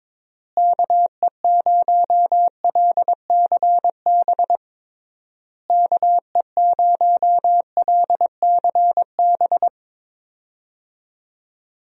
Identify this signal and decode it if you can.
Morse code, heard as a keyed tone, about 22 wpm, 700 Hz. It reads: KE0LCB KE0LCB
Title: USA Call Signs - 6 Characters - Repeated 3x - 22wpm